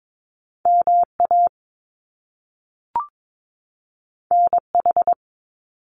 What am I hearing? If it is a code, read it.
Morse code, heard as a keyed tone, about 22 wpm, 700 Hz. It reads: MA NH
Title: US States 22wpm